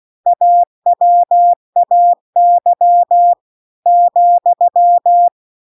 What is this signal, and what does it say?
Morse code, heard as a keyed tone, about 16 wpm, 700 Hz. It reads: AWAY ,